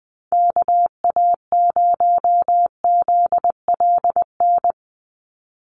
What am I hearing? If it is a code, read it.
Morse code, heard as a keyed tone, about 20 wpm, 700 Hz. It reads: KA0ZLN
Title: USA Call Signs 20wpm